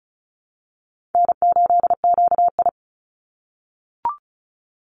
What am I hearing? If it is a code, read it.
Morse code, heard as a keyed tone, about 34 wpm, 700 Hz. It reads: N8QI
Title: USA Call Signs 35wpm